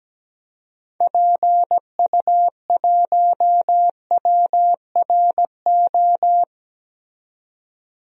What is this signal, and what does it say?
Morse code, heard as a keyed tone, about 17 wpm, 700 Hz. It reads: PU1WRO
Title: DXCC-100 Call Signs - 6 Characters - 17wpm